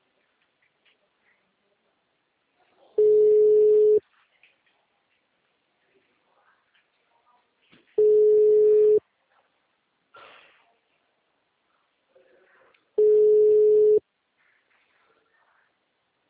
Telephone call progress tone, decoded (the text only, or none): ringback tone